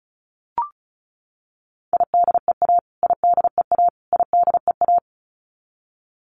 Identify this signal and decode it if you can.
Morse code, heard as a keyed tone, about 34 wpm, 700 Hz. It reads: IDEA IDEA IDEA
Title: Top 500 Words Repeated 35wpm